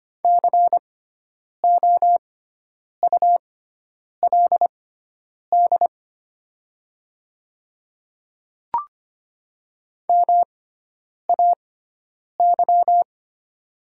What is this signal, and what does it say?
Morse code, heard as a keyed tone, about 25 wpm, spaced as for 9 wpm, 700 Hz. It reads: COULD MAY
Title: 6x Character Spacing - Top 100 Words - 25wpm